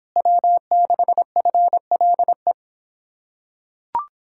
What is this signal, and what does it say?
Morse code, heard as a keyed tone, about 26 wpm, 700 Hz. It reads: W6FLE